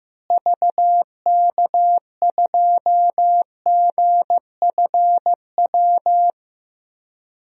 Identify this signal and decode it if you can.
Morse code, heard as a keyed tone, about 15 wpm, 700 Hz. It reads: VK2GFW
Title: DXCC-100 Call Signs - 6 Characters - Repeated 2x - 15wpm